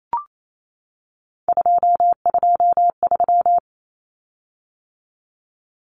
Morse code, heard as a keyed tone, about 28 wpm, 700 Hz. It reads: 223
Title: Common Cut Numbers - 3-Digit 28wpm